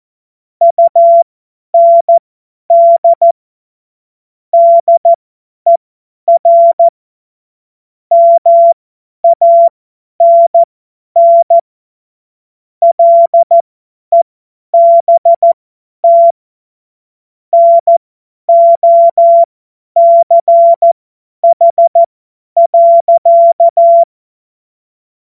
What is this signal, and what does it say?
Morse code, heard as a keyed tone, about 14 wpm, spaced as for 10 wpm, 680 Hz. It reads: UND DER MANN LEBT NOCH.
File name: Morsekurs / Witze / Witz 01. August 2023 50 F70